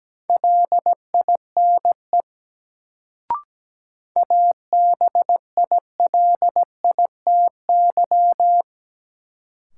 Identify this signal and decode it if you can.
Morse code, heard as a keyed tone, about 17 wpm, 700 Hz. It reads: LINE ABILITY
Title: Sentences from Top 1200 Words - New Word Drill Alt - 17wpm